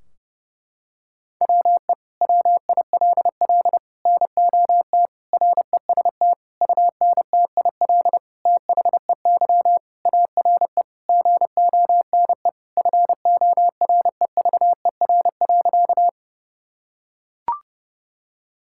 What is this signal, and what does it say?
Morse code, heard as a keyed tone, about 32 wpm, 700 Hz. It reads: WE WILL NOT REST UNTIL THEY ARE GONE FOREVER.
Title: Sentences from Top 1600 Words 30wpm